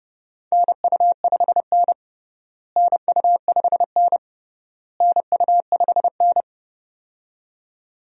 Morse code, heard as a keyed tone, about 30 wpm, 700 Hz. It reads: NU5N NU5N NU5N
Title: USA Call Signs - 4 Characters - Repeated 3x - 30wpm